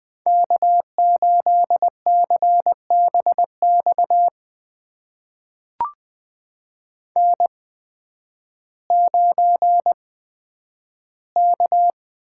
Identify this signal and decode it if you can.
Morse code, heard as a keyed tone, about 20 wpm, spaced as for 11 wpm, 700 Hz. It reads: K8CBX N9K
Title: USA Call Signs - 8x Character Spacing - 20wpm